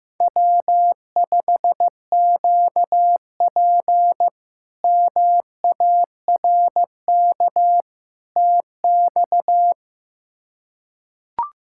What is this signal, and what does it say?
Morse code, heard as a keyed tone, about 15 wpm, 700 Hz. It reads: W5QP MARK TX
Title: K1USN SST QSO Practice - 15wpm